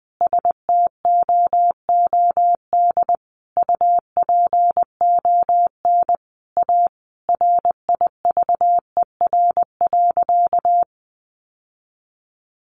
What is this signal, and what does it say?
Morse code, heard as a keyed tone, about 20 wpm, 700 Hz. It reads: STOOD UPON A RIVER.
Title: Sentences from Top 1100 Words 20wpm